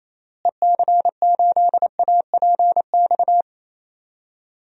Morse code, heard as a keyed tone, about 28 wpm, 700 Hz. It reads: EC8APX
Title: DXCC-100 Call Signs - 6 Characters - Repeated 3x - 28wpm